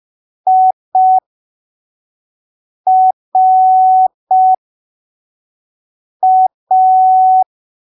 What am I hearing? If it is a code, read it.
Morse code, heard as a keyed tone, about 5 wpm, spaced as for 3 wpm, 750 Hz. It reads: IRA
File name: arquivos / texto / texto 30 05ppm